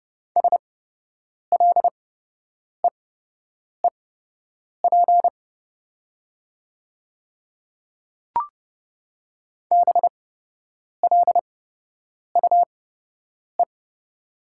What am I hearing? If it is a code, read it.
Morse code, heard as a keyed tone, about 30 wpm, spaced as for 8 wpm, 700 Hz. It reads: SLEEP BLUE
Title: 8x Character Spacing - Top 500 Words - 30wpm